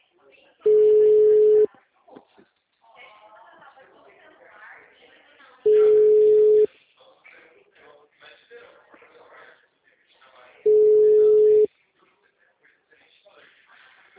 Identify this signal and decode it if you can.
Ringback tone (Europe, 425 Hz), with speech.